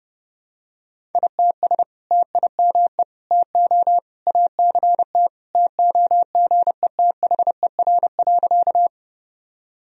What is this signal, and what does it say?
Morse code, heard as a keyed tone, about 30 wpm, 700 Hz. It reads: ITS TIME TO ACT TOGETHER.
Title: Sentences from Top 1000 Words - Review Part 1 - 30wpm